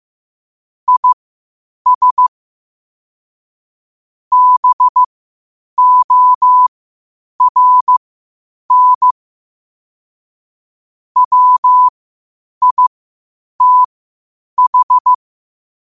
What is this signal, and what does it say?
Morse code, heard as a keyed tone, about 15 wpm, spaced as for 8 wpm, 1000 Hz. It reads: IS BORN WITH
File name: qotdfiles / 8wpm